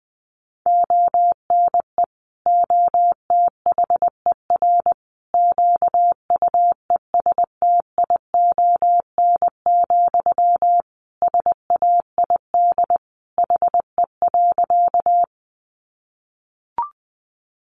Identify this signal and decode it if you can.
Morse code, heard as a keyed tone, about 20 wpm, 700 Hz. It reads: ONE OTHER QUESTION, SAID HE.
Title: Sentences from Top 400 Words 20wpm